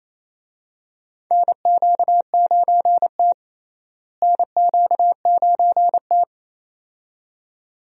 Morse code, heard as a keyed tone, about 28 wpm, 700 Hz. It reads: NQ9T NQ9T
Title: USA Call Signs - 4 Characters - Repeated 2x - 28wpm